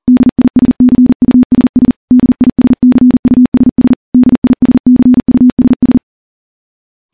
Morse code, heard as a keyed tone, about 40 wpm, 250 Hz. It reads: DISCUSS DISCUSS DISCUSS
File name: morse-training-40WPM